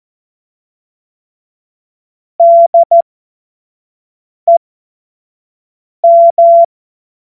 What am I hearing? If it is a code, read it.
Morse code, heard as a keyed tone, about 14 wpm, spaced as for 5 wpm, 680 Hz. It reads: DEM
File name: Morsekurs / Witze / Witz 14. Februar 2023 25 F70